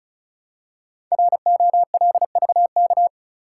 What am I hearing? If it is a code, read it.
Morse code, heard as a keyed tone, about 36 wpm, 700 Hz. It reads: ROLVK
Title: Code Groups - 5 Letter:Numbers - 35wpm